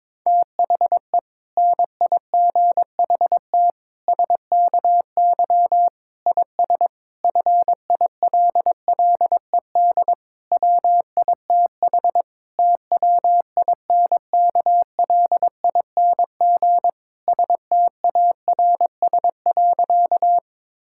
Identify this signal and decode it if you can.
Morse code, heard as a keyed tone, about 22 wpm, 700 Hz. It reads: THE NIGHT SKY IS FILLED WITH TWINKLING STARS.
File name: Difficult Sentences with Hints - 22wpm